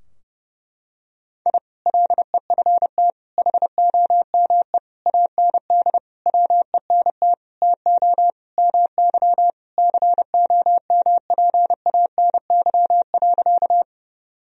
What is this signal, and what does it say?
Morse code, heard as a keyed tone, about 30 wpm, 700 Hz. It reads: I LEFT HOME AND WENT TO MY COMPANY.
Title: Sentences from Top 1000 Words - Review Part 3 - 30wpm